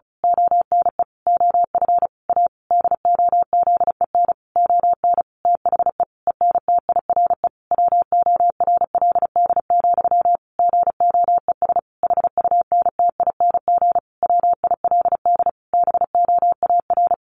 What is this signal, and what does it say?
Morse code, heard as a keyed tone, about 34 wpm, 700 Hz. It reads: ONE OF A DOZEN ON THE ENTIRE WORLD, GOES HUNTING WILD BOAR
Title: A Sound Of Thunder 35wpm